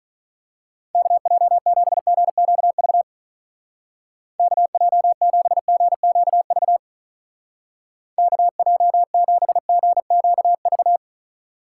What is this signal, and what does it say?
Morse code, heard as a keyed tone, about 42 wpm, 700 Hz. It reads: KJ7GQV KJ7GQV KJ7GQV
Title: Speed Racing - USA Call Signs 35wpm